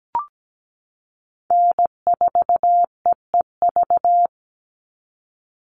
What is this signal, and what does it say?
Morse code, heard as a keyed tone, about 17 wpm, 700 Hz. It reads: N4EEV
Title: Call Signs Repeated 3x - 17wpm